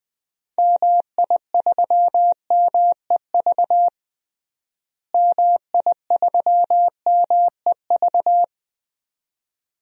Morse code, heard as a keyed tone, about 20 wpm, 700 Hz. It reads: MI3MEV MI3MEV
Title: DXCC-100 Call Signs - 6 Characters - Repeated 2x - 20wpm